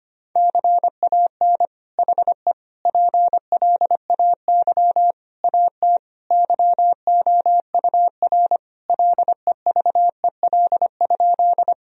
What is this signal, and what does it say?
Morse code, heard as a keyed tone, about 25 wpm, 700 Hz. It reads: CAN HE PLAY AT YOUR LEVEL?
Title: Sentences from Top 1000 Words - Review Part 4 - 25wpm